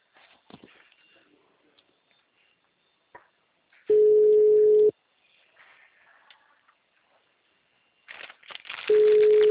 Ringback tone (Europe, 425 Hz).